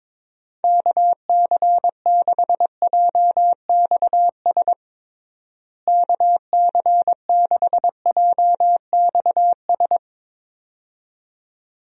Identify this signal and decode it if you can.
Morse code, heard as a keyed tone, about 22 wpm, 700 Hz. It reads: KC6JXS KC6JXS
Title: USA Call Signs - 6 Characters - Repeated 2x - 22wpm